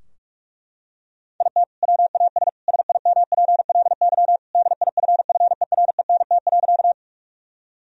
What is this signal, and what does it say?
Morse code, heard as a keyed tone, about 42 wpm, 700 Hz. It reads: IT WAS SIMPLY DIFFERENT.
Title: Sentences from Top 1000 Words - Review Part 4 - 45wpm